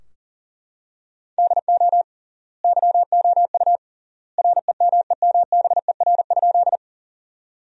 Morse code, heard as a keyed tone, about 37 wpm, 700 Hz. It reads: DO YOU REMEMBER?
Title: 3x Word-Spacing - Sentences from Top 400 Words - 40wpm